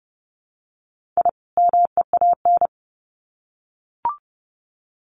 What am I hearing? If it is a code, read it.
Morse code, heard as a keyed tone, about 30 wpm, 700 Hz. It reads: I MEAN
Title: Sets of 2 Words 30wpm